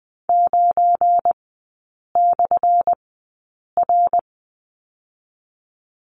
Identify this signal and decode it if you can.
Morse code, heard as a keyed tone, about 20 wpm, spaced as for 8 wpm, 700 Hz. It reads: 9/R
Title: 218 Lesson 35e - Number 0 - Three Characters 20wpm